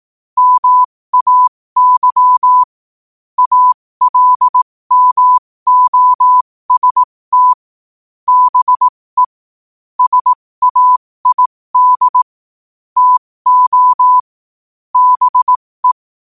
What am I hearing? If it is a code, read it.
Morse code, heard as a keyed tone, about 18 wpm, spaced as for 15 wpm, 1000 Hz. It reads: MAY ALMOST BE SAID TO BE
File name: qotdfiles / 15wpm